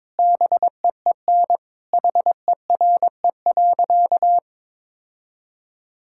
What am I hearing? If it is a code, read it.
Morse code, heard as a keyed tone, about 22 wpm, 700 Hz. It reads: BEEN HERE.
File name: practice / 22wpm / Sentences with 4-Word Phrases - 22wpm